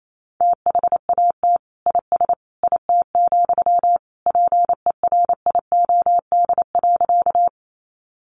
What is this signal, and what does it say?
Morse code, heard as a keyed tone, about 28 wpm, 700 Hz. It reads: THAT IS IT, PERIOD.
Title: Sentences from Top 800 Words 28wpm